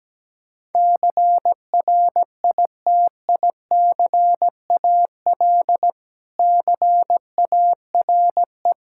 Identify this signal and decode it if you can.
Morse code, heard as a keyed tone, about 17 wpm, 700 Hz. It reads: CRITICAL CARE
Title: Practice QSO - Exchange 3 - More Info - 17wpm